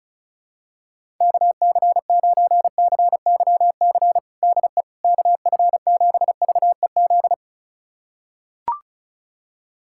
Morse code, heard as a keyed tone, about 34 wpm, 700 Hz. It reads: KC9CYC DE KF7VEZ
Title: USA Callsign DE Callsign 35wpm